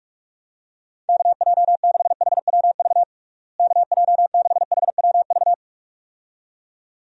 Morse code, heard as a keyed tone, about 46 wpm, 700 Hz. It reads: KJ6HWV KJ6HWV
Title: USA Call Signs - 6 Characters - Repeated 2x - 45wpm